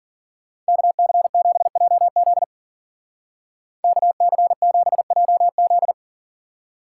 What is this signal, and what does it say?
Morse code, heard as a keyed tone, about 47 wpm, 700 Hz. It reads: KC7JZ KC7JZ
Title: Speed Racing - USA 2x2 Call Signs 40wpm